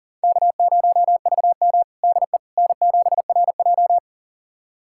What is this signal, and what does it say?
Morse code, heard as a keyed tone, about 40 wpm, 700 Hz. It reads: K0VM DE N7RJ
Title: USA Callsign DE Callsign - 4 Characters - 40wpm